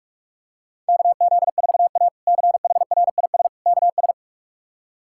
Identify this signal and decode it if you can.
Morse code, heard as a keyed tone, about 41 wpm, 700 Hz. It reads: KZ4A CHRIS KS
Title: K1USN SST QSO Practice - 45wpm